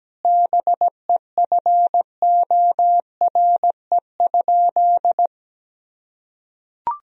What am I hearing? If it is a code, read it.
Morse code, heard as a keyed tone, about 17 wpm, 700 Hz. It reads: BEFORE?
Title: Sentences from Top 1000 Words - Review Part 3 - 17wpm